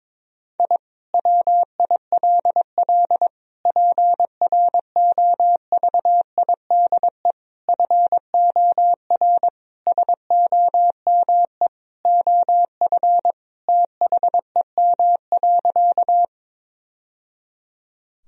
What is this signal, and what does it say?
Morse code, heard as a keyed tone, about 22 wpm, 700 Hz. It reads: I WILL PROVIDE FOR SOME OF THEM.
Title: Sentences from Top 1000 Words - Review Part 3 - 22wpm